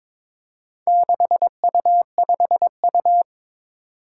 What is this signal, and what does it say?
Morse code, heard as a keyed tone, about 22 wpm, 700 Hz. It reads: 6U5U
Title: Four Challenging Letter-Number Combinations 22wpm